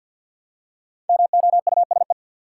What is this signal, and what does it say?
Morse code, heard as a keyed tone, about 50 wpm, 700 Hz. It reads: MOUSE
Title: Five Letter Words - 50wpm